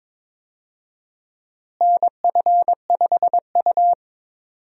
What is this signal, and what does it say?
Morse code, heard as a keyed tone, about 22 wpm, 700 Hz. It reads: NF5U
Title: USA Call Signs - 4 Characters - Repeated 3x - 22wpm